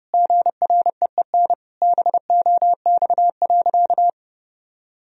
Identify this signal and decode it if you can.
Morse code, heard as a keyed tone, about 30 wpm, 700 Hz. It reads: GREEN BOX.
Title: Sentences from Top 1000 Words - Review Part 2 - 30wpm